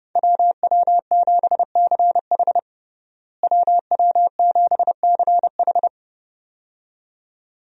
Morse code, heard as a keyed tone, about 30 wpm, 700 Hz. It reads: WW7CH WW7CH
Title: USA Call Signs - 5 Characters - Repeated 2x - 30wpm